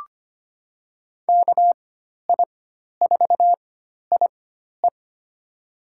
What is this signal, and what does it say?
Morse code, heard as a keyed tone, about 25 wpm, spaced as for 12 wpm, 700 Hz. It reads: KI4IE
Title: USA Call Signs - 5 Characters - 4x Character Spacing - 25wpm